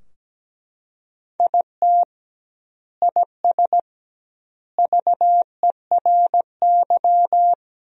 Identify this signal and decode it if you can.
Morse code, heard as a keyed tone, about 17 wpm, 700 Hz. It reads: IT IS VERY
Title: 2x Word-Spacing - Sentences from Top 500 Words - 17wpm